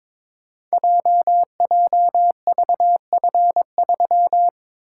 Morse code, heard as a keyed tone, about 22 wpm, 700 Hz. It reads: JJVF3
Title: Code Groups - 5 Letter:Numbers - 22wpm